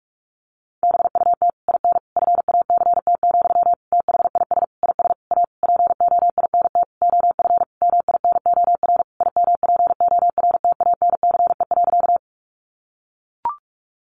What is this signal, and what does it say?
Morse code, heard as a keyed tone, about 44 wpm, 700 Hz. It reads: BUT IN FACT, THIS IS A POINT OF MINOR IMPORTANCE.
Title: Sentences from Top 1500 Words 45wpm